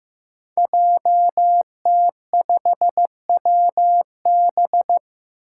Morse code, heard as a keyed tone, about 15 wpm, 700 Hz. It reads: JT5WB
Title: Five Challenging Letter-Number Combinations 15wpm